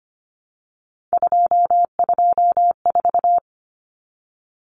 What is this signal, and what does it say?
Morse code, heard as a keyed tone, about 25 wpm, 700 Hz. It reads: 224
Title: Common Cut Numbers - 3-Digit 25wpm